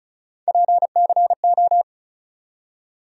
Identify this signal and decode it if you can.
Morse code, heard as a keyed tone, about 34 wpm, 700 Hz. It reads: PCO
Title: USA Call Sign Postfixes - 3 Letters 35wpm